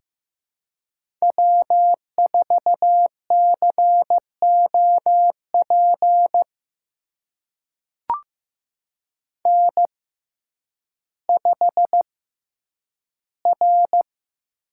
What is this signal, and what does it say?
Morse code, heard as a keyed tone, about 15 wpm, spaced as for 9 wpm, 700 Hz. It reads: W4COP N5R
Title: USA Call Signs - 5 Characters - 6x Character Spacing - 15wpm